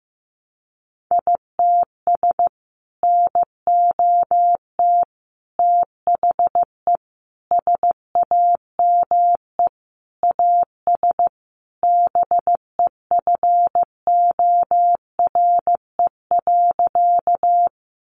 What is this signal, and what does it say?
Morse code, heard as a keyed tone, about 15 wpm, 700 Hz. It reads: ITS NOT THE SAME AS BEFORE.
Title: Sentences from Top 600 Words 15wpm